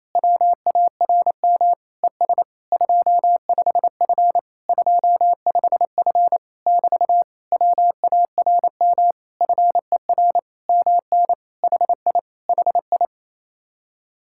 Morse code, heard as a keyed tone, about 28 wpm, 700 Hz. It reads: WARM ES 25F 25F <BT> WARM FER MN HI HI
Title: Speed-Racing Common QSO Phrases Encore - 28wpm